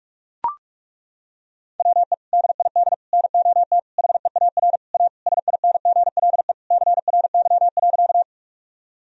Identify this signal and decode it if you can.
Morse code, heard as a keyed tone, about 42 wpm, 700 Hz. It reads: WE DID NOT HEAR A SINGLE CRY.
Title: Sentences from Top 1000 Words - Review Part 4 - 45wpm